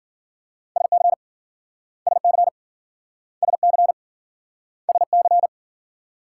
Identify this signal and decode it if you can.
Morse code, heard as a keyed tone, about 53 wpm, 700 Hz. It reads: SC SC SC SC